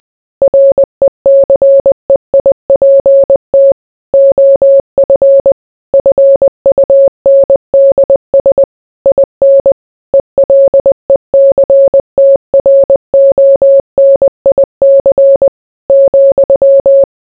Morse code, heard as a keyed tone, about 20 wpm, 550 Hz. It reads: RECEIPT OF FUNDS IN ELECTRONIC ,